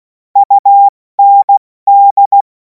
Morse code, heard as a keyed tone, about 16 wpm, 800 Hz. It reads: UND
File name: Morsekurs / Witze / Witz 24. Februar 2026 70 F80